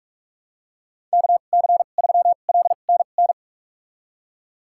Morse code, heard as a keyed tone, about 46 wpm, spaced as for 32 wpm, 690 Hz. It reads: KC3LNN
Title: USA Call Signs - 6 Characters - 2x Character Spacing - 45wpm